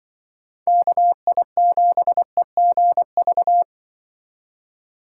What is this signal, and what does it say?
Morse code, heard as a keyed tone, about 23 wpm, 700 Hz. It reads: KI7EGV
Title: Speed Racing - USA 2x3 Call Signs 20wpm